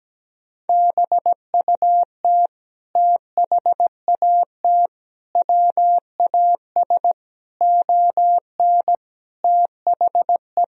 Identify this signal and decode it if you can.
Morse code, heard as a keyed tone, about 17 wpm, 700 Hz. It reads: BUT THAT WAS ON THE
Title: Sentences from Top 1000 Words - Review Part 1 - 17wpm